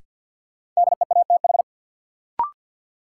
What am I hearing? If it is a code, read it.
Morse code, heard as a keyed tone, about 49 wpm, 700 Hz. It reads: DEATH
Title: Sentences from Top 1100 Words - New Word Drill Alt - 50wpm